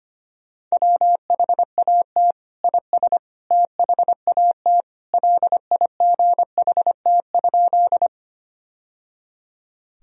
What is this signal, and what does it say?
Morse code, heard as a keyed tone, about 25 wpm, 700 Hz. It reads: WHAT IS THAT LIGHT?
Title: Sentences from Top 1000 Words - Review Part 1 - 25wpm